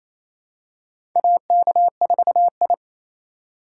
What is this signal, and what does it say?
Morse code, heard as a keyed tone, about 28 wpm, 700 Hz. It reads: AK4I
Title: USA Call Signs - 4 Characters - 2x Character Spacing - 28wpm